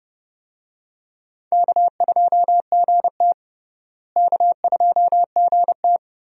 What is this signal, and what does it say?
Morse code, heard as a keyed tone, about 32 wpm, 700 Hz. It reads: K2GT K2GT
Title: USA Call Signs - 4 Characters - Repeated 3x - 30wpm